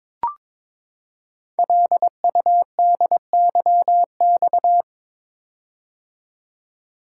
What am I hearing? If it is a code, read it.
Morse code, heard as a keyed tone, about 22 wpm, 700 Hz. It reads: LUDYX
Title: Code Groups - 5 Letters - 22wpm